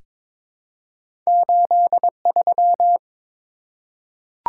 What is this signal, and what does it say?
Morse code, heard as a keyed tone, about 23 wpm, 700 Hz. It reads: 83
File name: practice / 22wpm / Two Letter-Number Combinations - 22wpm